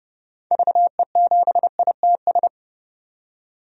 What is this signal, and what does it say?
Morse code, heard as a keyed tone, about 30 wpm, 700 Hz. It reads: VE7ITS
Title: DXCC-100 Call Signs - 6 Characters - 30wpm